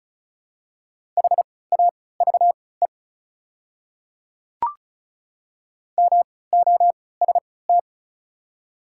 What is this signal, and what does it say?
Morse code, heard as a keyed tone, about 36 wpm, spaced as for 20 wpm, 690 Hz. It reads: HAVE MOST
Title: 3x Character Spacing - Top 100 Words - 35wpm